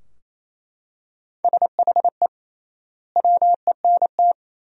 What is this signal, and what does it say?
Morse code, heard as a keyed tone, about 28 wpm, 700 Hz. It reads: SHE WENT
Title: 3x Word-Spacing - Sentences from Top 200 Words - 28wpm